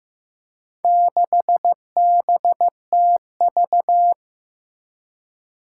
Morse code, heard as a keyed tone, about 15 wpm, 700 Hz. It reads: 6BTV
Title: Four Challenging Letter-Number Combinations 15wpm